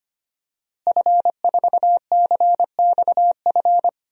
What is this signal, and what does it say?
Morse code, heard as a keyed tone, about 25 wpm, 700 Hz. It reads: F4CXF
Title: DXCC-100 Call Signs - 5 Characters - Repeated 3x - 25wpm